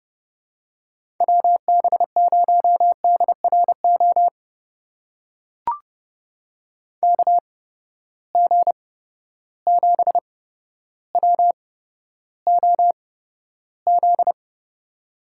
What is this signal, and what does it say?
Morse code, heard as a keyed tone, about 30 wpm, spaced as for 13 wpm, 700 Hz. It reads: WB0DRO KG7WOZ
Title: USA Call Signs - 6 Characters - 8x Character Spacing - 30wpm